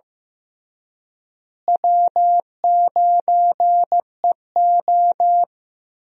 Morse code, heard as a keyed tone, about 15 wpm, 700 Hz. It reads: W9EO
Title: USA Call Signs - 4 Characters - Repeated 3x - 15wpm